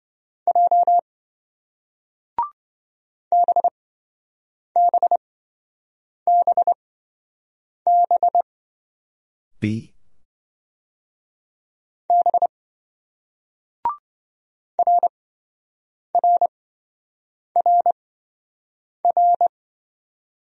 Morse code, heard as a keyed tone, about 28 wpm, spaced as for 7 wpm, 700 Hz. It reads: J BBBB B RRRR